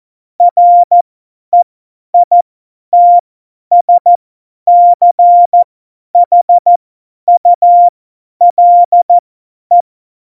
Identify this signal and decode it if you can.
Morse code, heard as a keyed tone, about 14 wpm, spaced as for 10 wpm, 700 Hz. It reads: REITSCHULE
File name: Morsekurs / Witze / Witz 12 Mai 2020, 50 F70